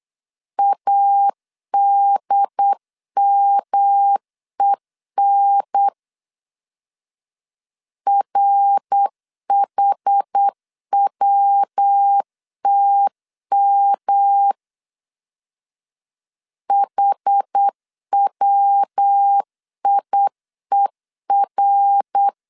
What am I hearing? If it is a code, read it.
Morse code, heard as a keyed tone, about 8 wpm, 800 Hz. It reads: ADMEN RHWTM HWIER
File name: CW / DET / 1 3 1500C S B 8wpm 4141